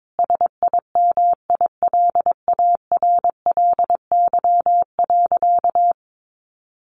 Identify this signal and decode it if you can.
Morse code, heard as a keyed tone, about 22 wpm, 700 Hz. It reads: SIMILARLY.
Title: Sentences from Top 1400 Words 22wpm